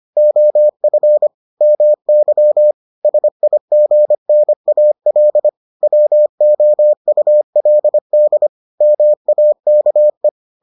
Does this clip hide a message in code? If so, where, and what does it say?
Yes: Morse code from 0.2 to 10.3 s, heard as a keyed tone, about 25 wpm, 600 Hz. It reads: OF MY SIGNAL WOULD MAKE